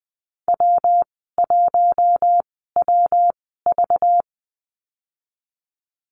Morse code, heard as a keyed tone, about 20 wpm, spaced as for 15 wpm, 700 Hz. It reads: W1WV
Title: 200 Lesson 32g - Letter k - Call Signs 20wpm